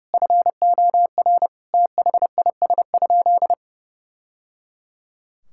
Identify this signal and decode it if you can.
Morse code, heard as a keyed tone, about 30 wpm, 700 Hz. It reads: FOR THIS?
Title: Sentences from Top 1000 Words - Review Part 5 - 30wpm